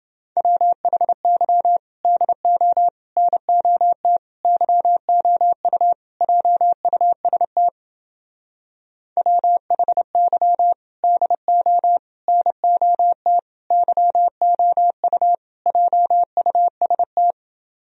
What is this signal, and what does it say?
Morse code, heard as a keyed tone, about 29 wpm, 700 Hz. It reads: WHY DO NOT YOU JUST WHY DO NOT YOU JUST